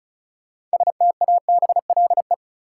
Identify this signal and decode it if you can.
Morse code, heard as a keyed tone, about 36 wpm, 700 Hz. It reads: STABLE
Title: Sentences from Top 1500 Words - New Word Drill Alt - 35wpm